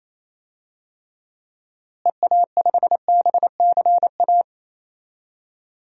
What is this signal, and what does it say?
Morse code, heard as a keyed tone, about 28 wpm, 700 Hz. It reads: EA5BCA
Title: DXCC-100 Call Signs - 6 Characters - Repeated 2x - 28wpm